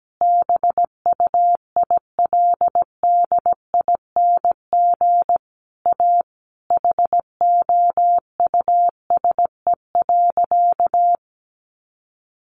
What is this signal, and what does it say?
Morse code, heard as a keyed tone, about 17 wpm, 700 Hz. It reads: BUILDING A HOUSE.
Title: Sentences from Top 700 Words 17wpm